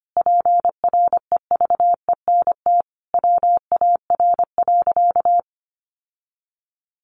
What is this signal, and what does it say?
Morse code, heard as a keyed tone, about 25 wpm, 700 Hz. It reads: PREVENT WAR.
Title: Sentences from Top 1100 Words 25wpm